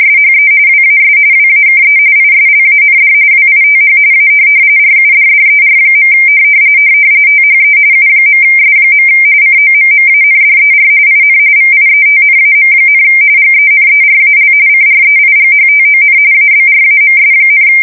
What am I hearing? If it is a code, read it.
RTTY, about 45.45 baud. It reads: SHE WAS WAKE ISLAND WITH A RUDDER. THE U.S. COMMANDER IN VIETNAM, GENERAL CREIGHTON ABRAMS,
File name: fsk170 / USS-New-Jersey